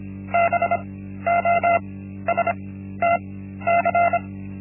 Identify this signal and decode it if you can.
Morse code, heard as a keyed tone, about 25 wpm, spaced as for 13 wpm, 690 Hz. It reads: BOSTC
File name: Linksmakalnis-1